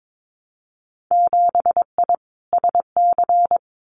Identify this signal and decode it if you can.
Morse code, heard as a keyed tone, about 22 wpm, 700 Hz. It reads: 7I SC
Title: Winter Field Day Exchanges Encore 22wpm